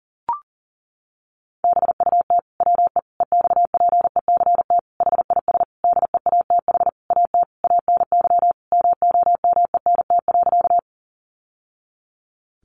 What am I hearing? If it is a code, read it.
Morse code, heard as a keyed tone, about 37 wpm, 700 Hz. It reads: BUT WE EXPECT HIS DEATH AT ANY MOMENT.
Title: Sentences from Top 1100 Words 40wpm